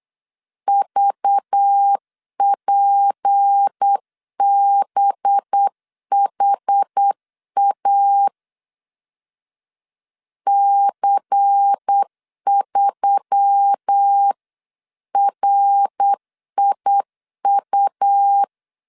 Morse code, heard as a keyed tone, about 8 wpm, 800 Hz. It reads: VPBHA C3RIU
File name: CW / DET / 1 6 1500C S B 8wpm 5142